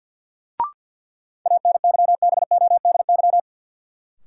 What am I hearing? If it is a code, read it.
Morse code, heard as a keyed tone, about 46 wpm, 700 Hz. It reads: ANYBODY
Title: Sets of 5 Words - New Word Drill - 50wpm